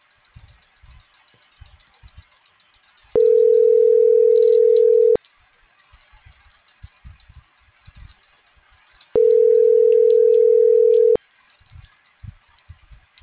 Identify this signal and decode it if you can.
Ringback tone (North America), 440 + 480 Hz.